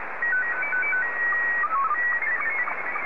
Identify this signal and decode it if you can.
SSTV (slow-scan television), Scottie 2.